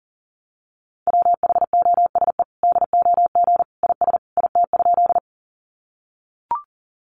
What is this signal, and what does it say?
Morse code, heard as a keyed tone, about 40 wpm, 700 Hz. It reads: WHOSE DOG IS IT?
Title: Sentences from Top 1200 Words 40wpm